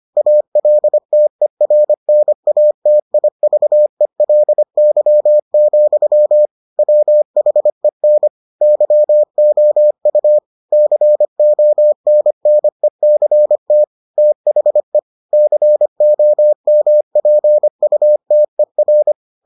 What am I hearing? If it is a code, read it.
Morse code, heard as a keyed tone, about 25 wpm, 600 Hz. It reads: ALTERNATIVELY, WHEN YOU CONNECT THE COMPUTER